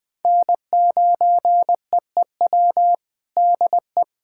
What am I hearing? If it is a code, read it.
Morse code, heard as a keyed tone, about 20 wpm, 700 Hz. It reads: N9EEW DE
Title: USA Callsign DE Callsign - 5 Characters - 20wpm